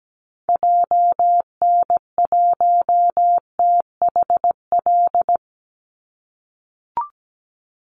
Morse code, heard as a keyed tone, about 17 wpm, 700 Hz. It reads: JN1THL
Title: Call Signs - 17wpm